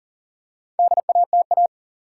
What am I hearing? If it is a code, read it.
Morse code, heard as a keyed tone, about 40 wpm, 700 Hz. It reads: DATA
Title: Four Letter Words - 40wpm